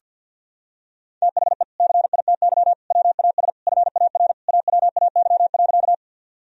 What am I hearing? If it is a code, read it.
Morse code, heard as a keyed tone, about 52 wpm, 700 Hz. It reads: THE CITY WAS FAR AWAY.
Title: Sentences from Top 1000 Words - Review Part 1 - 50wpm